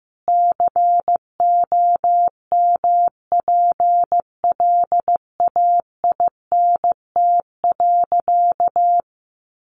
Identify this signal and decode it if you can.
Morse code, heard as a keyed tone, about 15 wpm, 700 Hz. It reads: COMPLAINT.
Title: Sentences from Top 1400 Words 15wpm